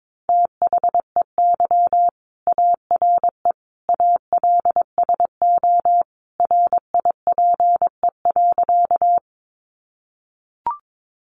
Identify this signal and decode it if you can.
Morse code, heard as a keyed tone, about 22 wpm, 700 Hz. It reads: THEY ARE ALSO RIPE.
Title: Additional Easy Phrases 22wpm